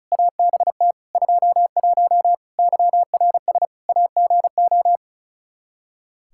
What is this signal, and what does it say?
Morse code, heard as a keyed tone, about 34 wpm, 700 Hz. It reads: ABT 21 YRS AGO
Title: Practice QSO - Exchange 3 - More Info - 35wpm